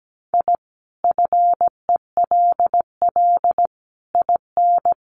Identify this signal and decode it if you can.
Morse code, heard as a keyed tone, about 17 wpm, 700 Hz. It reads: I FELL IN
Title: Sets of 5 Words Encore 17wpm